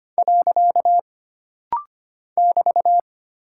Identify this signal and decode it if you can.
Morse code, heard as a keyed tone, about 25 wpm, 700 Hz. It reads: . <BT>
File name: Single Punctuation-Prosign - Rapid-Fire - 25wpm